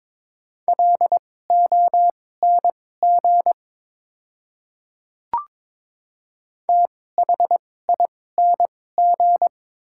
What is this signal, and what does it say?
Morse code, heard as a keyed tone, about 22 wpm, spaced as for 16 wpm, 700 Hz. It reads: LONG THING
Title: 2x Character Spacing - Top 500 Words - 22wpm